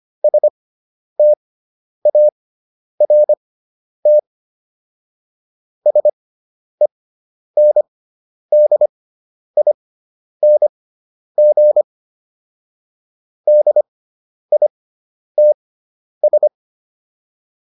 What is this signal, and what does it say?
Morse code, heard as a keyed tone, about 25 wpm, spaced as for 10 wpm, 600 Hz. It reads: START SENDING DITS